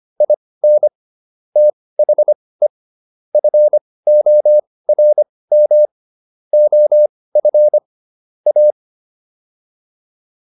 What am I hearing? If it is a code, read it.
Morse code, heard as a keyed tone, about 25 wpm, spaced as for 18 wpm, 600 Hz. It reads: IN THE FORM OF A